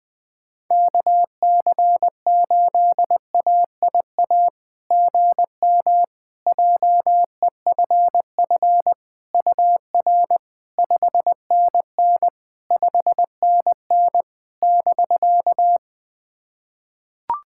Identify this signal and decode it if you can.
Morse code, heard as a keyed tone, about 20 wpm, 700 Hz. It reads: KC8AIA GM JEFF UR 5NN 5NN <BK>
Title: SOTA QSO Practice 20wpm